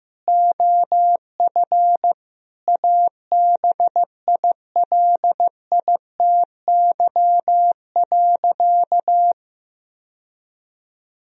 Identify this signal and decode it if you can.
Morse code, heard as a keyed tone, about 15 wpm, 700 Hz. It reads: OF ABILITY.
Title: Sentences from Top 1000 Words - Review Part 3 - 15wpm